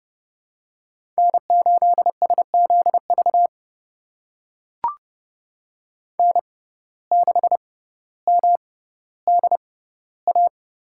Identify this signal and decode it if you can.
Morse code, heard as a keyed tone, about 30 wpm, spaced as for 15 wpm, 700 Hz. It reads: N8SZV N6MDA
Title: USA Call Signs - 6x Character Spacing - 30wpm